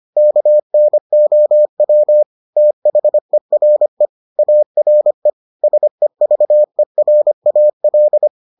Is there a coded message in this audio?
Yes: Morse code from 0.2 to 8.3 s, heard as a keyed tone, about 25 wpm, 600 Hz. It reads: KNOW THERE ARE SEVERAL